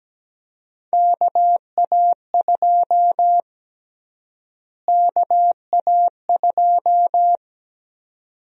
Morse code, heard as a keyed tone, about 17 wpm, 700 Hz. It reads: KA2 KA2
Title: USA Call Sign Prefixes - Repeated 2x - 17wpm